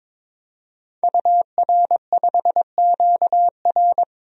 Morse code, heard as a keyed tone, about 22 wpm, 700 Hz. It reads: UR5QR
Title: DXCC-100 Call Signs - 5 Characters - 22wpm